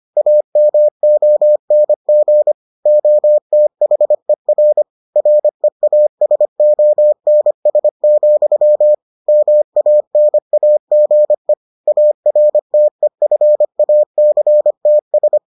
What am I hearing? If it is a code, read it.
Morse code, heard as a keyed tone, about 25 wpm, 600 Hz. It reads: AMONG OTHER REASONS, MANAGE ARTEFACTS